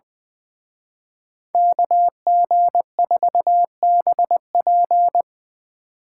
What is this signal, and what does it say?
Morse code, heard as a keyed tone, about 24 wpm, 700 Hz. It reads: KG4BP
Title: Speed Racing - USA 2x2 Call Signs 20wpm